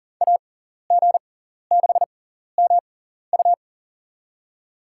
Morse code, heard as a keyed tone, about 42 wpm, spaced as for 14 wpm, 700 Hz. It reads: AG6MU
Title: USA Call Signs - 5 Characters - 6x Character Spacing - 40wpm